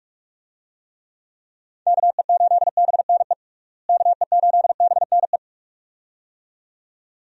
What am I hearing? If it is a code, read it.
Morse code, heard as a keyed tone, about 46 wpm, 700 Hz. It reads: KE8BNE KE8BNE
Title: USA Call Signs - 6 Characters - Repeated 2x - 45wpm